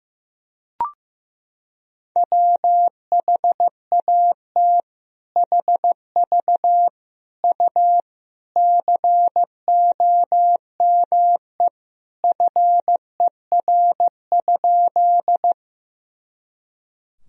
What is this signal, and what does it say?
Morse code, heard as a keyed tone, about 15 wpm, 700 Hz. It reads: WHAT HV U COME FER?
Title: Sentences from Top 100 Words and Abbreviations 15wpm